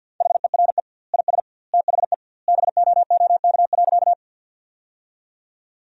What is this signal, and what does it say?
Morse code, heard as a keyed tone, about 50 wpm, 700 Hz. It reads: HERE IS THE BOOK.
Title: 2x Word-Spacing - Sentences from Top 300 Words - 50wpm